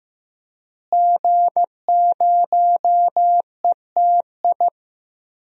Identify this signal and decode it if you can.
Morse code, heard as a keyed tone, about 15 wpm, 700 Hz. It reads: G0ETI
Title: DXCC-100 Call Signs - 5 Characters - Repeated 3x - 15wpm